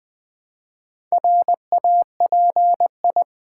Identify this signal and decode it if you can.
Morse code, heard as a keyed tone, about 20 wpm, 700 Hz. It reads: RAPI
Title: Build A Word - 1 Letter At A Time - 5-Letter Words - 20wpm